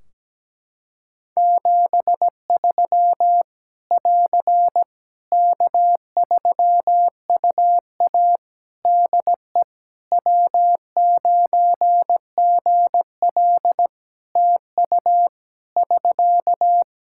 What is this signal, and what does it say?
Morse code, heard as a keyed tone, about 17 wpm, 700 Hz. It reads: 73 <AR> K3UA DE W9GL TU <SK>
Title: Practice QSO - Exchange 4 - Ending - 17wpm